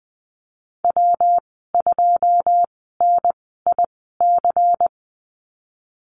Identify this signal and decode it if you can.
Morse code, heard as a keyed tone, about 20 wpm, spaced as for 15 wpm, 700 Hz. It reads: W2NIC
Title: 186 Lesson 30g - Letter b - Call Signs 20wpm